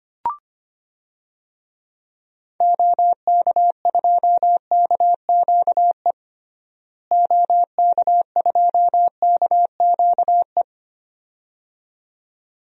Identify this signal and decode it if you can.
Morse code, heard as a keyed tone, about 25 wpm, 700 Hz. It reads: OK2KQE OK2KQE
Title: DXCC-100 Call Signs - 6 Characters - Repeated 2x - 25wpm